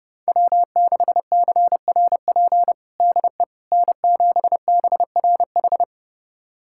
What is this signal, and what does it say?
Morse code, heard as a keyed tone, about 32 wpm, 700 Hz. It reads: W6CRP DE N7BRH
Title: USA Callsign DE Callsign - 5 Characters - 30wpm